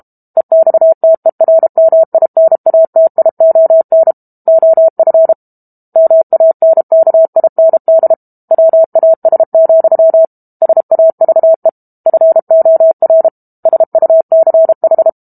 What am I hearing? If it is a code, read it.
Morse code, heard as a keyed tone, about 32 wpm, 650 Hz. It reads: EXTERMINATION OF MANKIND WAS, SAVE FOR SUCH
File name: Book Two / 06-Chapter 6 - 32 WPM